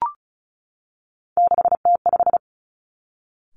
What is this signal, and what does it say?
Morse code, heard as a keyed tone, about 36 wpm, 700 Hz. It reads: 6T5
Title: Three Challenging Letter-Number Combinations 35wpm